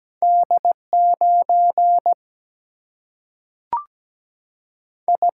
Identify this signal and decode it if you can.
Morse code, heard as a keyed tone, about 17 wpm, 700 Hz. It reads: D9 I
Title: Single-Letter Single-Number 17wpm